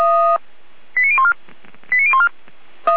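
Five-tone selective call (ZVEI), heard as quick runs of five tones, tones of about 70 ms. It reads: 89014 89014
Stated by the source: ZVEI five-tone selective call received on SDR.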